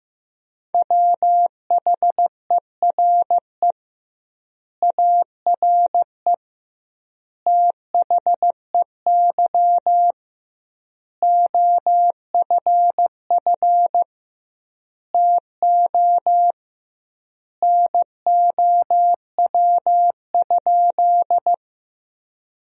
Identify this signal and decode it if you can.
Morse code, heard as a keyed tone, about 16 wpm, 700 Hz. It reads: WHERE ARE THEY OFF TO NOW?
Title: 2x Word-Spacing - Sentences from Top 200 Words - 15wpm